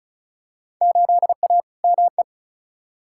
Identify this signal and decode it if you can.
Morse code, heard as a keyed tone, about 34 wpm, 700 Hz. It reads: 8A ME
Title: Field Day Exchanges - 35wpm